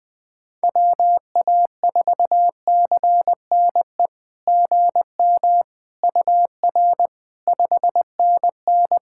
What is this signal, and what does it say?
Morse code, heard as a keyed tone, about 20 wpm, 700 Hz. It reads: WA4CNE GM UR 5NN
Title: SOTA QSO Practice 20wpm